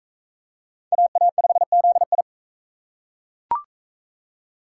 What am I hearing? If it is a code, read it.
Morse code, heard as a keyed tone, about 41 wpm, 700 Hz. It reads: AA5ZI